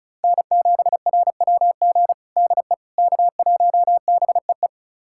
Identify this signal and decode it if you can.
Morse code, heard as a keyed tone, about 36 wpm, 700 Hz. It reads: N7RWG DE K1BEE
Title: USA Callsign DE Callsign - 5 Characters - 35wpm